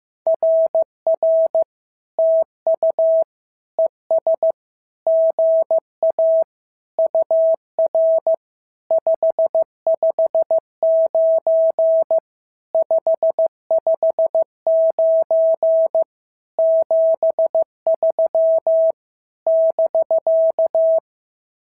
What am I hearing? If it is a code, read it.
Morse code, heard as a keyed tone, about 14 wpm, 650 Hz. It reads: RR TU ES GA UR 559 559 73 <BK>
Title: SOTA QSO Practice 15wpm